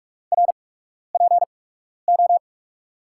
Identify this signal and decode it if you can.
Morse code, heard as a keyed tone, about 43 wpm, spaced as for 12 wpm, 700 Hz. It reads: RPO
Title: USA Call Signs - 5 Characters - 8x Character Spacing - 45wpm